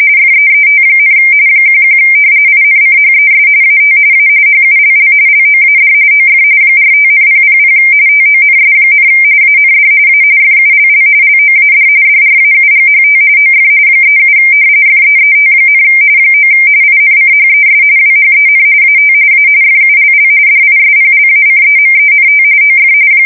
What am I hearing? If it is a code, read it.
RTTY, about 45.45 baud. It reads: FEET, 11 INCHES AT A HULL LOAD OF 45,000 TONS, AND A FULL LOAD DRAUGHT OF 37 FEET, 9 INCHES WHEN AT A FULL DISPLACEMENT OF
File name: fsk170 / battleshipnj